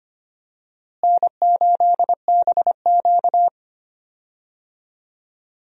Morse code, heard as a keyed tone, about 25 wpm, 700 Hz. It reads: N8BQ
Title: USA Call Signs - 4 Characters - Repeated 3x - 25wpm